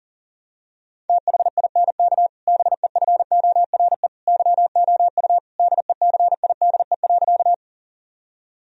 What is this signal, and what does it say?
Morse code, heard as a keyed tone, about 40 wpm, 700 Hz. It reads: THINK BEFORE YOU DECIDE.
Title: Sentences from Top 1000 Words - Review Part 3 - 40wpm